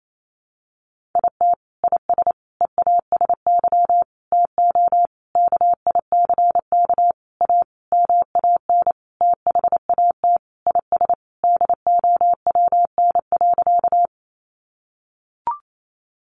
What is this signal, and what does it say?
Morse code, heard as a keyed tone, about 28 wpm, 700 Hz. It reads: IT IS EASY TO KICK A MAN THAT IS DOWN.
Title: Sentences from Top 1300 Words 28wpm